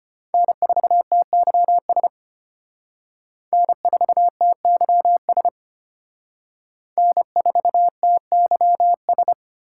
Morse code, heard as a keyed tone, about 30 wpm, 700 Hz. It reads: N4TYS N4TYS N4TYS
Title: Speed Racing - USA 1x3 Call Signs 25wpm